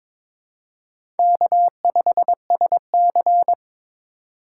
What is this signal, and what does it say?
Morse code, heard as a keyed tone, about 22 wpm, 700 Hz. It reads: K5SC
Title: USA Call Signs - 4 Characters - 2x Character Spacing - 22wpm